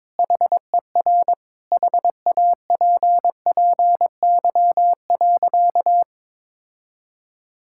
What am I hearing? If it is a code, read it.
Morse code, heard as a keyed tone, about 22 wpm, 700 Hz. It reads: HER HAPPY.
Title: Sentences from Top 1000 Words - Review Part 6 - 22wpm